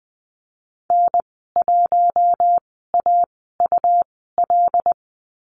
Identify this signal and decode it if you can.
Morse code, heard as a keyed tone, about 20 wpm, spaced as for 15 wpm, 700 Hz. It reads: N1AUL
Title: 147 Lesson 24g - Letter y - Call Signs 20wpm